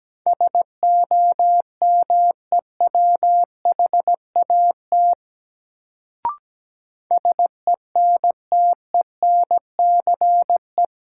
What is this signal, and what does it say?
Morse code, heard as a keyed tone, about 17 wpm, 700 Hz. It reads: SOMEWHAT SENTENCE
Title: Sentences from Top 1000 Words - Review Part 1 - New Word Drill - 17wpm